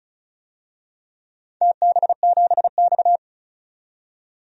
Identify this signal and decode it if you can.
Morse code, heard as a keyed tone, about 36 wpm, 700 Hz. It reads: TB7X
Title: DXCC-100 Call Signs - 4 Characters - 35wpm